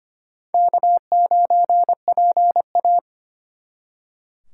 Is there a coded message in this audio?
Yes: Morse code from 0.5 to 3.0 s, heard as a keyed tone, about 34 wpm, 700 Hz. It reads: K9PA